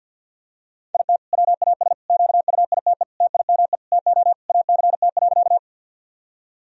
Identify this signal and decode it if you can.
Morse code, heard as a keyed tone, about 46 wpm, 700 Hz. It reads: IT WAS QUITE TIME TO ACT.
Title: Sentences from Top 1000 Words - Review Part 1 - 50wpm